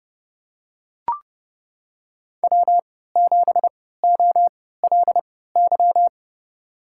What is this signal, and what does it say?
Morse code, heard as a keyed tone, about 30 wpm, spaced as for 17 wpm, 700 Hz. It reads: W7OLY
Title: USA Call Signs - 5 Characters - 3x Character Spacing - 30wpm